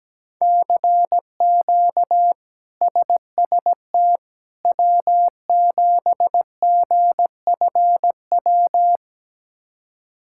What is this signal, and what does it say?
Morse code, heard as a keyed tone, about 17 wpm, 700 Hz. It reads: CQ SST W7GFW
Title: K1USN SST QSO Practice - 17wpm